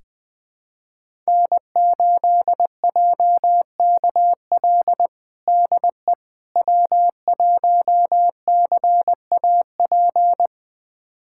Morse code, heard as a keyed tone, about 20 wpm, 700 Hz. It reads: N8JKL DE W1CAP
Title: USA Callsign DE Callsign - 5 Characters - 20wpm